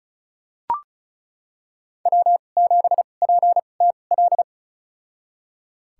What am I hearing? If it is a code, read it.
Morse code, heard as a keyed tone, about 36 wpm, spaced as for 25 wpm, 700 Hz. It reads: W7PTL
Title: USA Call Signs - 5 Characters - 2x Character Spacing - 35wpm